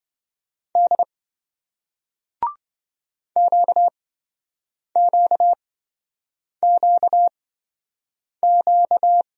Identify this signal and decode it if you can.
Morse code, heard as a keyed tone, about 26 wpm, spaced as for 7 wpm, 700 Hz. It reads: D QQQQ